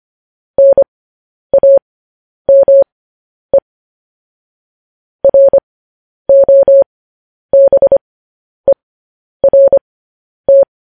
Morse code, heard as a keyed tone, about 25 wpm, spaced as for 10 wpm, 550 Hz. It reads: NAME ROBERT
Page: news-headlines-in-morse-code-at-10-wpm